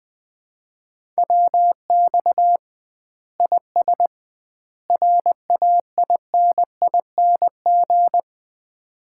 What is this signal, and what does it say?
Morse code, heard as a keyed tone, about 20 wpm, 700 Hz. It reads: WX IS RAINING
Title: Common QSO Phrases - 2x Word-Spacing - 20wpm